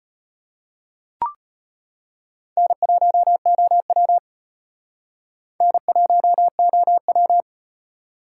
Morse code, heard as a keyed tone, about 38 wpm, 700 Hz. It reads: N1OW N1OW